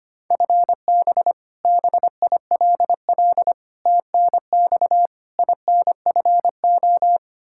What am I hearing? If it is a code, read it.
Morse code, heard as a keyed tone, about 25 wpm, 700 Hz. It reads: FB BILL TNX INFO
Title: Practice QSO - Exchange 3 - More Info - 25wpm